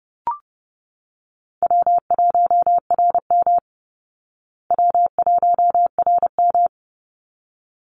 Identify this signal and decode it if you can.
Morse code, heard as a keyed tone, about 30 wpm, 700 Hz. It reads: W1RM W1RM
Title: Easy Call Signs Repeated 2x - 30wpm